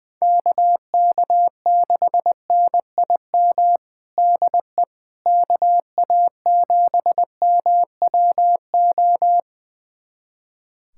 Morse code, heard as a keyed tone, about 20 wpm, 700 Hz. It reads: KK6NIM DE KA7MWO
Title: USA Callsign DE Callsign - 6 Characters - 20wpm